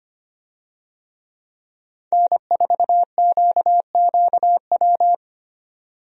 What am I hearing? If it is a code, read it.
Morse code, heard as a keyed tone, about 25 wpm, 700 Hz. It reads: N4QQW
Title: USA Call Signs - 5 Characters - 25wpm